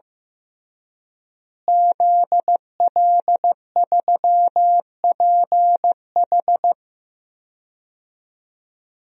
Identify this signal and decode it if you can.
Morse code, heard as a keyed tone, about 15 wpm, 700 Hz. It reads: ZL3PH
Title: DXCC-100 Call Signs - 5 Characters - Repeated 2x - 15wpm